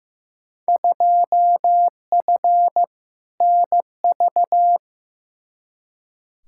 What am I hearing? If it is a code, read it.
Morse code, heard as a keyed tone, about 15 wpm, 700 Hz. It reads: 2F NV
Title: Field Day Exchanges - 15wpm